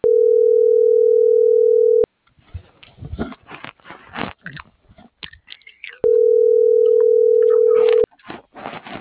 Ringback tone (North America), 440 + 480 Hz.